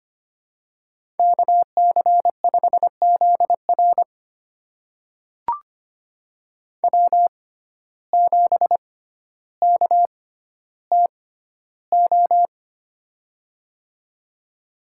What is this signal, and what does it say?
Morse code, heard as a keyed tone, about 25 wpm, spaced as for 13 wpm, 700 Hz. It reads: KC5ZR W7KTO
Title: USA Call Signs - 6x Character Spacing - 25wpm